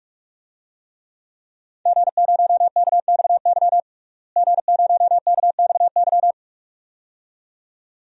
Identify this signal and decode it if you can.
Morse code, heard as a keyed tone, about 43 wpm, 700 Hz. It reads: G0KXY G0KXY
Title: DXCC-100 Call Signs - 5 Characters - Repeated 2x - 45wpm